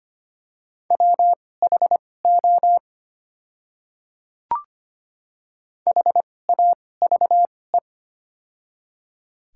Morse code, heard as a keyed tone, about 25 wpm, spaced as for 18 wpm, 700 Hz. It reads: WHO HAVE
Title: 2x Character Spacing - Top 100 Words - 25wpm